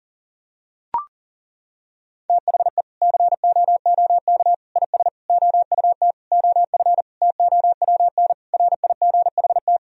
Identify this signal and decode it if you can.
Morse code, heard as a keyed tone, about 40 wpm, 700 Hz. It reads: THE COOK IS OUT OF TOWN RIGHT
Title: Sentences from Top 1000 Words - Review Part 4 - 40wpm